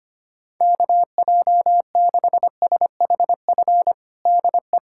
Morse code, heard as a keyed tone, about 25 wpm, 700 Hz. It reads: KJ6SHF DE
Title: USA Callsign DE Callsign 25wpm